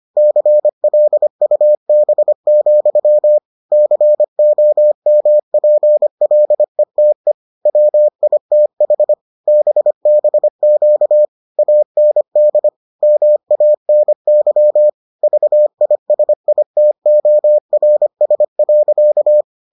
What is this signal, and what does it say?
Morse code, heard as a keyed tone, about 25 wpm, 600 Hz. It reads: CLUB, COMPLETE WITH BBQ AND MANY VISITORS.